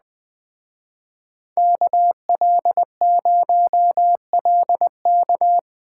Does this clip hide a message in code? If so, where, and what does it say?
Yes: Morse code from 1.6 to 5.6 s, heard as a keyed tone, about 20 wpm, 700 Hz. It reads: KL0LK